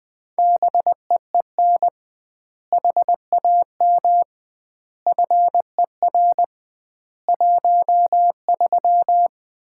Morse code, heard as a keyed tone, about 21 wpm, 700 Hz. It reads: BEEN HAM FER 13
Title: Common QSO Phrases - 2x Word-Spacing - 20wpm